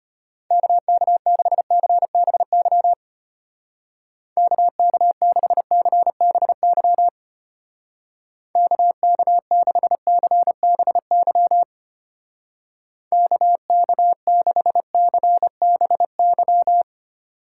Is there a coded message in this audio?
Yes: Morse code from 0.5 to 16.8 s, heard as a keyed tone, about 32 wpm, 700 Hz. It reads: KK6CBY KK6CBY KK6CBY KK6CBY